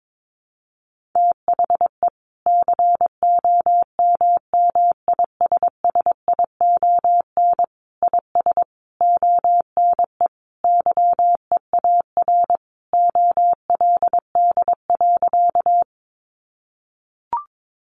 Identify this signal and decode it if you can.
Morse code, heard as a keyed tone, about 22 wpm, 700 Hz. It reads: THE COMMISSION IS ONE YEAR OLD.
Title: Sentences from Top 1400 Words 22wpm